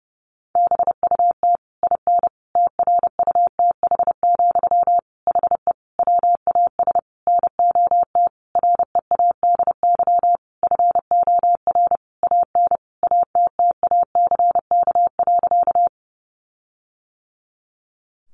Morse code, heard as a keyed tone, about 30 wpm, 700 Hz. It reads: BUT IN TRUTH, HE WAS NOT READY FOR AN ATTACK.
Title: Sentences from Top 1500 Words 30wpm